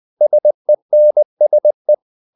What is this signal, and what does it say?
Morse code, heard as a keyed tone, about 20 wpm, 600 Hz. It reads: SENSE
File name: the old man and the sea / the old man and the sea part 1 20wpm